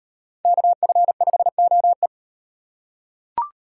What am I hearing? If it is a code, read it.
Morse code, heard as a keyed tone, about 36 wpm, 700 Hz. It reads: KF5OE